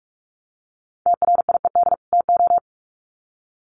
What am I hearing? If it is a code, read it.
Morse code, heard as a keyed tone, about 44 wpm, 700 Hz. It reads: TRIED TO
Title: Sets of 2 Words 45wpm